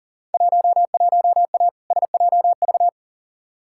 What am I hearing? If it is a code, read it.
Morse code, heard as a keyed tone, about 40 wpm, 700 Hz. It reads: 11A SJV
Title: Field Day Exchanges Encore - 40wpm